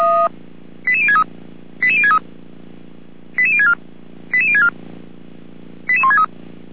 Five-tone selective call (ZVEI), heard as quick runs of five tones, tones of about 70 ms. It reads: 88063 88063 88064 88064 88163
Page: SDR/ZVEI